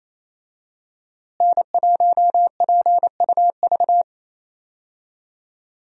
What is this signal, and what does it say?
Morse code, heard as a keyed tone, about 28 wpm, 700 Hz. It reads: N1PUV
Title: USA Call Signs - 5 Characters - Repeated 2x - 28wpm